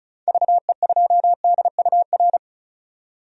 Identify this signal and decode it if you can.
Morse code, heard as a keyed tone, about 36 wpm, 700 Hz. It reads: VE2DUR
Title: DXCC-100 Call Signs - 6 Characters - 35wpm